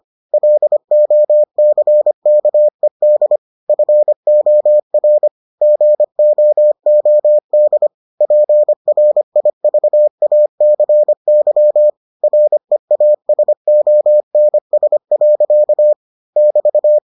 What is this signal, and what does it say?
Morse code, heard as a keyed tone, about 25 wpm, 600 Hz. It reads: LOCKED FOR GOOD PRIVACY REASONS. <BT>